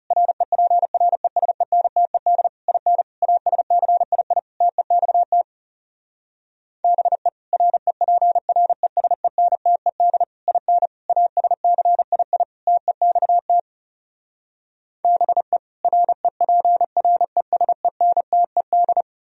Morse code, heard as a keyed tone, about 40 wpm, 700 Hz. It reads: REPRESENTED IN ASCII TEXT BE REPRESENTED IN ASCII TEXT BE REPRESENTED